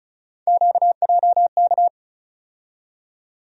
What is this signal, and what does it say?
Morse code, heard as a keyed tone, about 36 wpm, 700 Hz. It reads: QJK
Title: USA Call Sign Postfixes - 3 Letters 35wpm